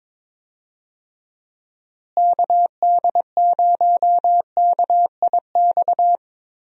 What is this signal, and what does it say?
Morse code, heard as a keyed tone, about 22 wpm, 700 Hz. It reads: KD0KIX
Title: USA Call Signs - 6 Characters - Repeated 3x - 22wpm